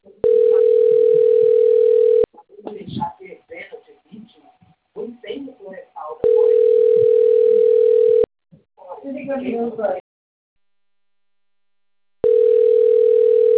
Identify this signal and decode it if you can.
Ringback tone (North America), 440 + 480 Hz, with speech.
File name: rg-1600-unknown- - - .27680